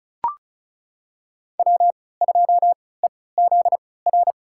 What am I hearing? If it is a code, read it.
Morse code, heard as a keyed tone, about 36 wpm, spaced as for 20 wpm, 690 Hz. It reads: W2EZR
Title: USA Call Signs - 5 Characters - 3x Character Spacing - 35wpm